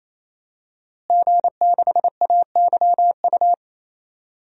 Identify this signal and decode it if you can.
Morse code, heard as a keyed tone, about 27 wpm, 700 Hz. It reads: G6AYU
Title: DXCC-100 Call Signs - 5 Characters - Repeated 3x - 28wpm